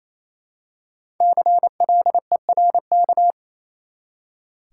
Morse code, heard as a keyed tone, about 28 wpm, 700 Hz. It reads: CLERK
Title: Five Letter Words - 28wpm